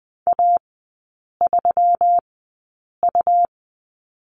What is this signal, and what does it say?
Morse code, heard as a keyed tone, about 20 wpm, spaced as for 8 wpm, 700 Hz. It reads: A3U
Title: 127 Lesson 21e - Number 6 - Three Characters 20wpm